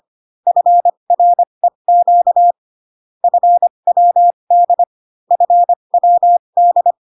Morse code, heard as a keyed tone, about 25 wpm, spaced as for 22 wpm, 700 Hz. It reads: FREQ FWD FWD
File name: CW common abbreviations alphabetical repeated 3 times 25 18wpm